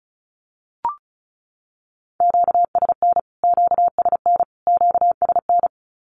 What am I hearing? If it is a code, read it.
Morse code, heard as a keyed tone, about 34 wpm, 700 Hz. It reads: QSN QSN QSN
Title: Q-Codes Repeated 3x 35wpm